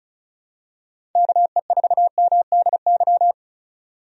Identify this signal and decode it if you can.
Morse code, heard as a keyed tone, about 36 wpm, 700 Hz. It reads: KE4MDY
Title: USA Call Signs - 6 Characters - 35wpm